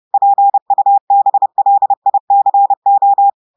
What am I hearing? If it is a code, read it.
Morse code, heard as a keyed tone, about 30 wpm, 800 Hz. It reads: PUBLICO
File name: portugais texte en clair 2 vitesse 30 mots minutes